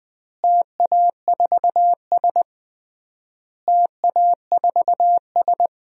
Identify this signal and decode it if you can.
Morse code, heard as a keyed tone, about 20 wpm, 700 Hz. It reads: TA4S TA4S
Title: DXCC-100 Call Signs - 4 Characters - Repeated 3x - 20wpm